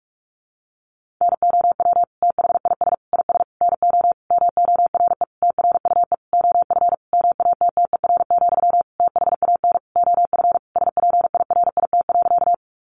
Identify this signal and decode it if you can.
Morse code, heard as a keyed tone, about 44 wpm, 700 Hz. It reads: NOW THIS IS NO MORE TRUE OF MATTER, THAN OF SPIRIT.
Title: Sentences from Top 1400 Words 45wpm